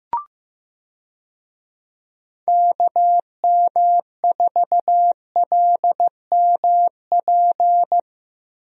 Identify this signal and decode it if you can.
Morse code, heard as a keyed tone, about 15 wpm, 700 Hz. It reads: KM4LMP
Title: USA Call Signs - 6 Characters - 15wpm